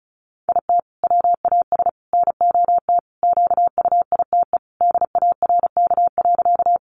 Morse code, heard as a keyed tone, about 34 wpm, 700 Hz. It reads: IT WAS NOT QUITE DARK.
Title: Sentences from Top 700 Words 35wpm